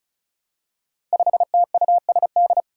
Morse code, heard as a keyed tone, about 34 wpm, 700 Hz. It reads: 5TUSD
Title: Five Challenging Letter-Number Combinations 35wpm